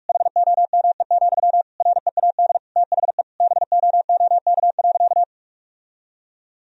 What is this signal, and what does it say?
Morse code, heard as a keyed tone, about 46 wpm, 700 Hz. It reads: HOME, READ THE BOOK.
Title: Sentences from Top 1000 Words - Review Part 2 - 45wpm